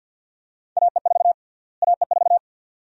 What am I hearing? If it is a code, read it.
Morse code, heard as a keyed tone, about 50 wpm, 700 Hz. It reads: AE4 AE4
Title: USA Call Sign Prefixes - Repeated 2x - 50wpm